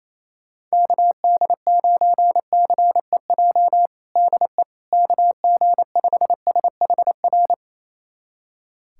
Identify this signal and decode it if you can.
Morse code, heard as a keyed tone, about 28 wpm, 700 Hz. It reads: KD9CEJ DE KG5SHR
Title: USA Callsign DE Callsign - 6 Characters - 28wpm